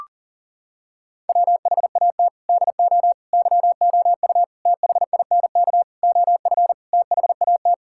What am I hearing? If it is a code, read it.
Morse code, heard as a keyed tone, about 40 wpm, 700 Hz. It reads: WHAT DO YOU THINK OF THAT
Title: Sentences from Top 1000 Words - Review Part 6 - 40wpm